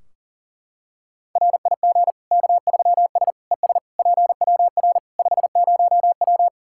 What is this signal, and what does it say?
Morse code, heard as a keyed tone, about 40 wpm, 700 Hz. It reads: RIG K3S ES PWR 50W
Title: Practice QSO - Exchange 2 - Info - 40wpm